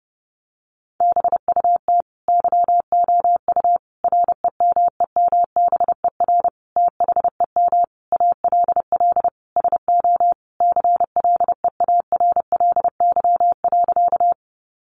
Morse code, heard as a keyed tone, about 30 wpm, 700 Hz. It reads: BUT YOU REMEMBER THEM ALL SO CLEARLY.
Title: Sentences from Top 1300 Words 30wpm